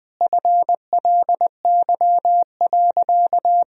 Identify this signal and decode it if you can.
Morse code, heard as a keyed tone, about 19 wpm, 700 Hz. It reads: FLY.
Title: Sentences from Top 1000 Words - Review Part 2 - 20wpm